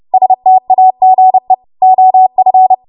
Morse code, heard as a keyed tone, about 30 wpm, 750 Hz. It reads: STAGE OF
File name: Morse / 30WPM